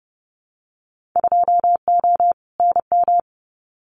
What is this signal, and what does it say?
Morse code, heard as a keyed tone, about 30 wpm, 700 Hz. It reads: 2O NM
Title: Winter Field Day Exchanges 30wpm